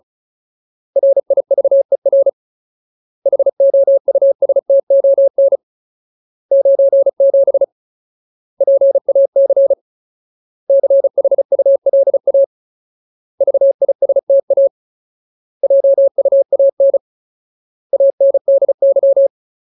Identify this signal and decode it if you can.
Morse code, heard as a keyed tone, about 36 wpm, 550 Hz. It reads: RIVER HOUSTON 97 PAC CHULA VISTA JUAN ANDY
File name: Ultimate Word List / 35WPM Ultimate